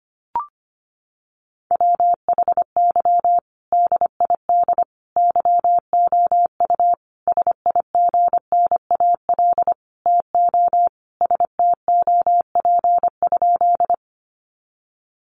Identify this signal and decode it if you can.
Morse code, heard as a keyed tone, about 25 wpm, 700 Hz. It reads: WHY DID YOU SIGNAL TO STOP?
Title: Sentences from Top 1400 Words 25wpm